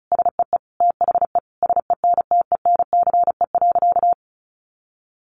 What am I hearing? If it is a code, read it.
Morse code, heard as a keyed tone, about 34 wpm, 700 Hz. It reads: SEE THE SENTENCE.
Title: Additional Easy Phrases 35wpm